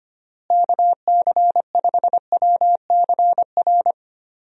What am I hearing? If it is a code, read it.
Morse code, heard as a keyed tone, about 25 wpm, 700 Hz. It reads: KC5WCR
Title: USA Call Signs - 6 Characters - Repeated 2x - 25wpm